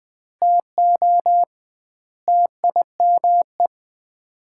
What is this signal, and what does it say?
Morse code, heard as a keyed tone, about 21 wpm, 700 Hz. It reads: TO TIME
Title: 2x Word-Spacing - Sets of 4 Words - 20wpm